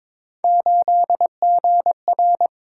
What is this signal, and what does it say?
Morse code, heard as a keyed tone, about 22 wpm, 700 Hz. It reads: 8GR
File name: practice / 22wpm / Three Letter-Number Combinations - 22wpm